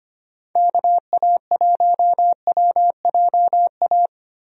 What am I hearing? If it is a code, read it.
Morse code, heard as a keyed tone, about 25 wpm, 700 Hz. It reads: KA1WJA
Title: USA Call Signs - 6 Characters - Repeated 3x - 25wpm